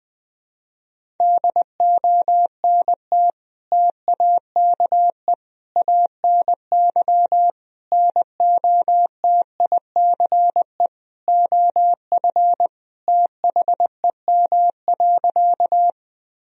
Morse code, heard as a keyed tone, about 20 wpm, 700 Hz. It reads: DONT TAKE ANY NOTICE OF THEM.
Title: Sentences from Top 1000 Words - Review Part 2 - 20wpm